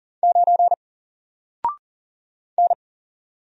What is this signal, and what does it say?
Morse code, heard as a keyed tone, about 40 wpm, 700 Hz. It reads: 9 N
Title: Single Letter-Number Alt 40wpm